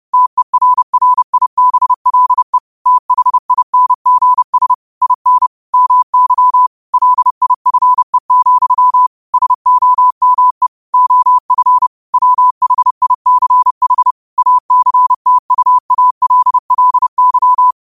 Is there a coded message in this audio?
Yes: Morse code from 0.1 to 17.7 s, heard as a keyed tone, about 30 wpm, 1000 Hz. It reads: TERRIBLE THINGS IN MY LIFE, SOME OF WHICH ACTUALLY